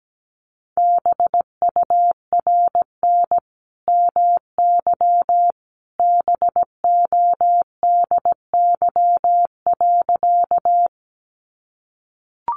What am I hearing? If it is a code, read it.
Morse code, heard as a keyed tone, about 17 wpm, 700 Hz. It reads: BURN MY BODY.
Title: Sentences from Top 1300 Words 17wpm